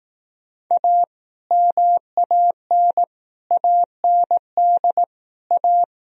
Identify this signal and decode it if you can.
Morse code, heard as a keyed tone, about 18 wpm, 700 Hz. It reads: A MAN AND A
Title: Speed-Racing - Sets of 5 Words 15wpm